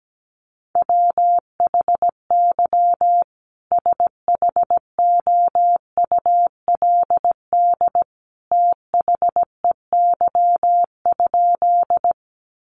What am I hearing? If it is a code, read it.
Morse code, heard as a keyed tone, about 17 wpm, 700 Hz. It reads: WHY SHOULD THEY?
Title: Sentences from Top 300 Words 17wpm